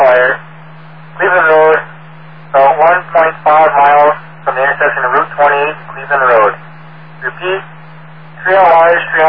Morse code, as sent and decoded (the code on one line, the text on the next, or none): none